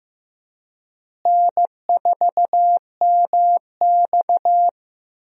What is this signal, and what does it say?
Morse code, heard as a keyed tone, about 15 wpm, 700 Hz. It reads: N4MX
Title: USA Call Signs - 4 Characters - Repeated 3x - 15wpm